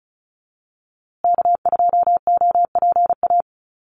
Morse code, heard as a keyed tone, about 34 wpm, 700 Hz. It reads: K2OPA
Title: USA Call Signs Repeated 3x 35wpm